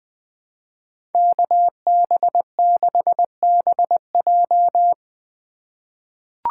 Morse code, heard as a keyed tone, about 20 wpm, 700 Hz. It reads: KB6BJ
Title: USA Call Signs - 5 Characters - 3x Character Spacing - 20wpm